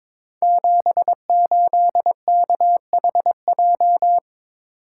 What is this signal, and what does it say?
Morse code, heard as a keyed tone, about 23 wpm, 700 Hz. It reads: 78KHJ
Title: Code Groups - 5 Letter:Numbers - 22wpm